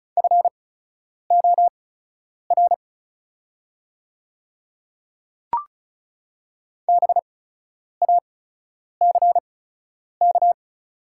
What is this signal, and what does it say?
Morse code, heard as a keyed tone, about 36 wpm, spaced as for 10 wpm, 700 Hz. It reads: FOR BACK
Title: 8x Character Spacing - Top 200 Words - 35wpm